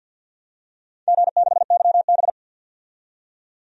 Morse code, heard as a keyed tone, about 46 wpm, 700 Hz. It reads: G6YB
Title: DXCC-100 Call Signs - 4 Characters - 50wpm